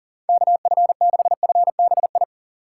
Morse code, heard as a keyed tone, about 40 wpm, 700 Hz. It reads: KF6FBI
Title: USA Call Signs - 6 Characters - 3x Character Spacing - 40wpm